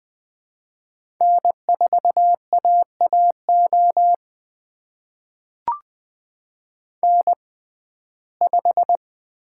Morse code, heard as a keyed tone, about 20 wpm, spaced as for 14 wpm, 700 Hz. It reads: N4AAO N5
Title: USA Call Signs - 5 Characters - 6x Character Spacing - 20wpm